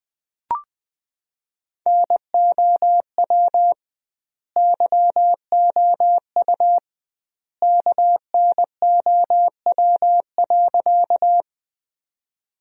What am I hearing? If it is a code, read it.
Morse code, heard as a keyed tone, about 20 wpm, 700 Hz. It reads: NOW YOU KNOW.
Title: 2x Word-Spacing - Sentences from Top 100 Words - 20wpm